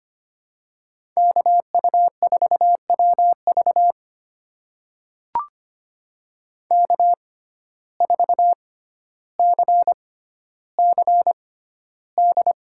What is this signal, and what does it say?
Morse code, heard as a keyed tone, about 25 wpm, spaced as for 13 wpm, 700 Hz. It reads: KU4WV K4CCD
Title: USA Call Signs - 5 Characters - 6x Character Spacing - 25wpm